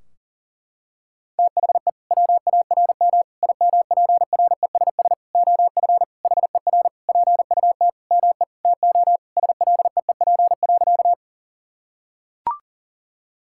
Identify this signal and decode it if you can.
Morse code, heard as a keyed tone, about 40 wpm, 700 Hz. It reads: THE WARM IMPRESS OF HER PUT ME TO SLEEP.
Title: Sentences from Top 1600 Words 40wpm